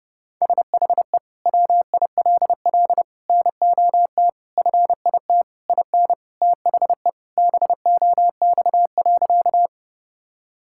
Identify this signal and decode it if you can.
Morse code, heard as a keyed tone, about 30 wpm, 700 Hz. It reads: SHE WILL NOT FIT IN THE BOX.
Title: Sentences from Top 1000 Words - Review Part 5 - 30wpm